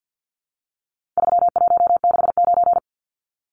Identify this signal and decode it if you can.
Morse code, heard as a keyed tone, about 50 wpm, 700 Hz. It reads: 3169
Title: Numbers - 4-Digit 50wpm